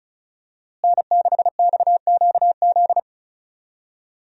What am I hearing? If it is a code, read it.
Morse code, heard as a keyed tone, about 34 wpm, 700 Hz. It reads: N6XQZ
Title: USA Call Signs - 5 Characters - Repeated 2x - 35wpm